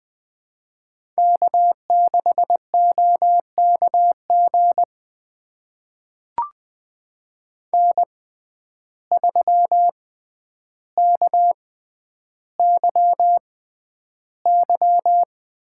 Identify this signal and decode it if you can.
Morse code, heard as a keyed tone, about 21 wpm, spaced as for 10 wpm, 700 Hz. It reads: K6OKG N3KYY
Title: USA Call Signs - 6x Character Spacing - 20wpm